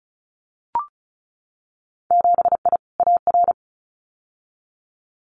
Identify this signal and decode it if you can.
Morse code, heard as a keyed tone, about 34 wpm, 700 Hz. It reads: 7I AR
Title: Winter Field Day Exchanges Encore 35wpm